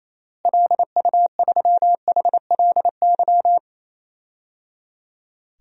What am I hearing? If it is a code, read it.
Morse code, heard as a keyed tone, about 28 wpm, 700 Hz. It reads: LU3HLY
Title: DXCC-100 Call Signs - 6 Characters - Repeated 3x - 28wpm